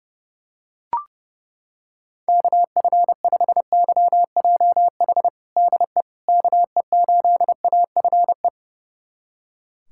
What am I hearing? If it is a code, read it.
Morse code, heard as a keyed tone, about 30 wpm, 700 Hz. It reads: KF5YJH DE KE8AFE
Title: USA Callsign DE Callsign - 6 Characters - 30wpm